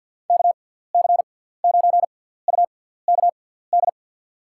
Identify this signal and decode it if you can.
Morse code, heard as a keyed tone, about 52 wpm, spaced as for 17 wpm, 700 Hz. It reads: KC9UKD
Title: USA Call Signs - 6 Characters - 6x Character Spacing - 50wpm